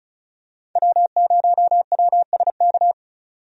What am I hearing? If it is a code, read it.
Morse code, heard as a keyed tone, about 36 wpm, 700 Hz. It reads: W0WSK
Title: USA Call Signs - 5 Characters - 35wpm